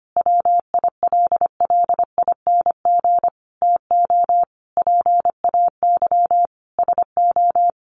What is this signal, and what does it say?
Morse code, heard as a keyed tone, about 25 wpm, 700 Hz. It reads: WILLING TO PAY SO
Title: Sentences from Top 1400 Words 25wpm